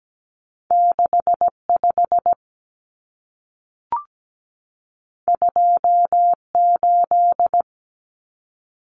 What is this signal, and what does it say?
Morse code, heard as a keyed tone, about 17 wpm, 700 Hz. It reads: 65 28
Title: Numbers - 2-Digit 17wpm